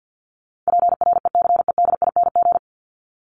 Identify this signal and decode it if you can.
Morse code, heard as a keyed tone, about 48 wpm, 700 Hz. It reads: PRECEDING
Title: ING Suffix - Top 100 Words with Base Words 50wpm